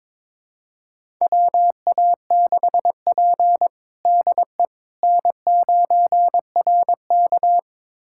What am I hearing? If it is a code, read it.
Morse code, heard as a keyed tone, about 22 wpm, 700 Hz. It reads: WA6P DE N9RK
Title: USA Callsign DE Callsign - 4 Characters - 22wpm